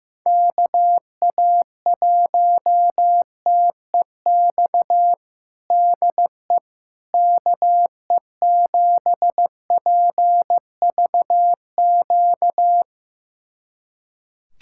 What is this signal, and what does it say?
Morse code, heard as a keyed tone, about 15 wpm, 700 Hz. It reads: KA1TEX DE KE7PVQ
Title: USA Callsign DE Callsign 15wpm